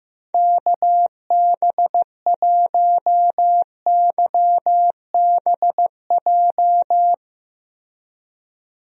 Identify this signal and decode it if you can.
Morse code, heard as a keyed tone, about 15 wpm, 700 Hz. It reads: KB1YBJ
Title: USA Call Signs - 6 Characters - Repeated 2x - 15wpm